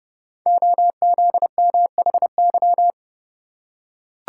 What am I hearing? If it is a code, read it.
Morse code, heard as a keyed tone, about 29 wpm, 690 Hz. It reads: OZMHY
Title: Code Groups - 5 Letters - 30wpm